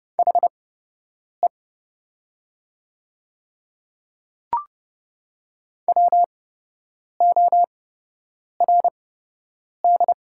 Morse code, heard as a keyed tone, about 30 wpm, spaced as for 8 wpm, 700 Hz. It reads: HE WORD
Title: 8x Character Spacing - Top 200 Words - 30wpm